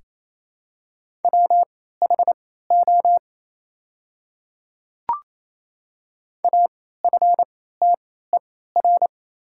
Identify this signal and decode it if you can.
Morse code, heard as a keyed tone, about 28 wpm, spaced as for 16 wpm, 700 Hz. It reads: WHO AFTER
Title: 3x Character Spacing - Top 200 Words - 28wpm